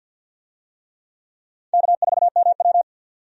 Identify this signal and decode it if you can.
Morse code, heard as a keyed tone, about 53 wpm, 700 Hz. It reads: K4MW
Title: USA Call Signs - 4 Characters - 50wpm